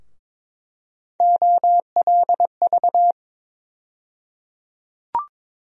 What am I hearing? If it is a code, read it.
Morse code, heard as a keyed tone, about 22 wpm, 700 Hz. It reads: OLV
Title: USA Call Sign Postfixes - 3 Letters 22wpm